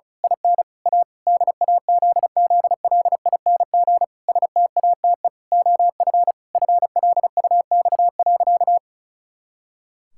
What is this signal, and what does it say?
Morse code, heard as a keyed tone, about 34 wpm, 700 Hz. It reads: IN A DAZZLING STATE OF FLUX.
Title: Pangram Challenge v2 - 35wpm